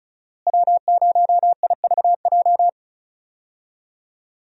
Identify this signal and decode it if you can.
Morse code, heard as a keyed tone, about 36 wpm, 700 Hz. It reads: W0IVJ
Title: USA Call Signs - 5 Characters - Repeated 2x - 35wpm